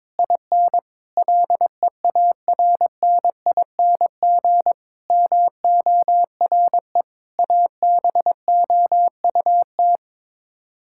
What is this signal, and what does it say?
Morse code, heard as a keyed tone, about 22 wpm, 700 Hz. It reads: IN LEARNING MORE ABOUT
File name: practice / 22wpm / Sets of 8 Words - 22wpm